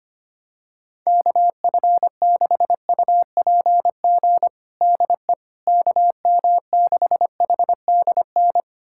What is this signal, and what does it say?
Morse code, heard as a keyed tone, about 25 wpm, 700 Hz. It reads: KF6UPG DE KM6HDN
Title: USA Callsign DE Callsign - 6 Characters - 25wpm